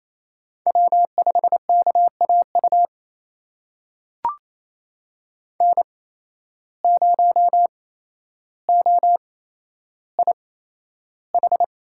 Morse code, heard as a keyed tone, about 27 wpm, spaced as for 12 wpm, 700 Hz. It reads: W5KAU N0OIH
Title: USA Call Signs - 5 Characters - 8x Character Spacing - 28wpm